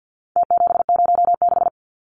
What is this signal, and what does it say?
Morse code, heard as a keyed tone, about 50 wpm, 700 Hz. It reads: T706
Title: Numbers - Contest Style - 4-Digit 50wpm